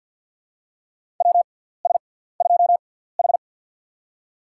Morse code, heard as a keyed tone, about 47 wpm, spaced as for 17 wpm, 700 Hz. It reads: WS2H
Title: USA Call Signs - 6x Character Spacing - 50wpm